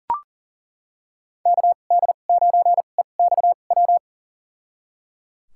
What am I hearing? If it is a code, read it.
Morse code, heard as a keyed tone, about 38 wpm, spaced as for 28 wpm, 700 Hz. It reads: KD9EXW
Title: USA Call Signs - 2x Character Spacing - 40wpm